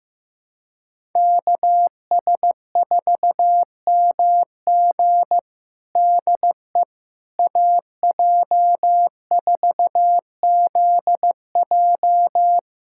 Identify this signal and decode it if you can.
Morse code, heard as a keyed tone, about 16 wpm, 700 Hz. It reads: KS4MG DE AJ4ZJ
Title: USA Callsign DE Callsign - 5 Characters - 15wpm